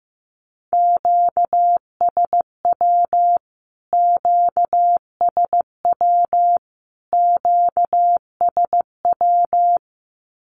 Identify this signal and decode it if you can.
Morse code, heard as a keyed tone, about 15 wpm, 700 Hz. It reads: QSW QSW QSW
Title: Q-Codes Repeated 3x 15wpm